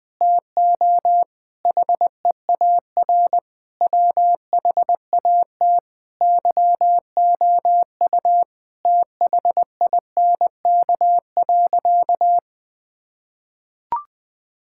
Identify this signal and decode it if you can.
Morse code, heard as a keyed tone, about 21 wpm, 700 Hz. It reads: TO HEAR WHAT YOU THINK.
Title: Sentences with 6-Word Phrases - Spoken Beginning - 20wpm